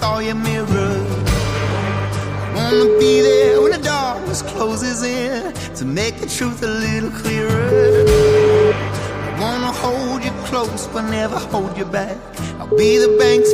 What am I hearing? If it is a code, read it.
Ringback tone (Europe, 425 Hz), with speech.